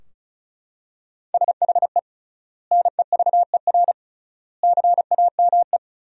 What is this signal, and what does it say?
Morse code, heard as a keyed tone, about 36 wpm, 700 Hz. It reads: SHE NEVER CAME
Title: 3x Word-Spacing - Sentences from Top 300 Words - 35wpm